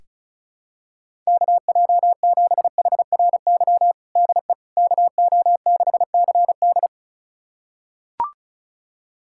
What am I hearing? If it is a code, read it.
Morse code, heard as a keyed tone, about 36 wpm, 700 Hz. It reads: KJ7HRY DE KO6CD
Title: USA Callsign DE Callsign 35wpm